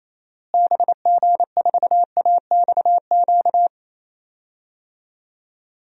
Morse code, heard as a keyed tone, about 28 wpm, 700 Hz. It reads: BG4AXQ
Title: DXCC-100 Call Signs - 6 Characters - Repeated 2x - 28wpm